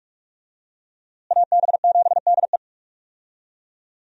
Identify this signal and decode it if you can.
Morse code, heard as a keyed tone, about 42 wpm, 700 Hz. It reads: AB7DE
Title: USA Call Signs - 5 Characters - Repeated 3x - 45wpm